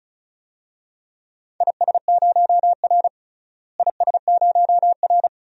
Morse code, heard as a keyed tone, about 36 wpm, 700 Hz. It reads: IS0R IS0R
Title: DXCC-100 Call Signs - 4 Characters - Repeated 2x - 35wpm